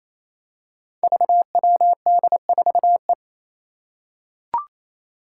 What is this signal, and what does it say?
Morse code, heard as a keyed tone, about 28 wpm, 700 Hz. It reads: VWD4E
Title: Five Challenging Letter-Number Combinations 28wpm